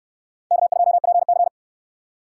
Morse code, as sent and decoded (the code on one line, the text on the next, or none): -.- ..--- .--. -.-.
K2PC